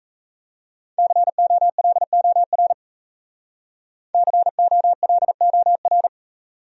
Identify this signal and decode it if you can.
Morse code, heard as a keyed tone, about 35 wpm, 690 Hz. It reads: COLOR COLOR
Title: Speed-Racing Top 500 Words 28wpm